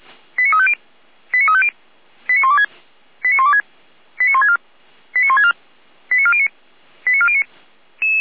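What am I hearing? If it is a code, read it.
Five-tone selective call (ZVEI), heard as quick runs of five tones, tones of about 70 ms. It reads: 89350 89350 89126 89126 89164 89164 89409 89409
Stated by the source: ZVEI five-tone selective call received on SDR.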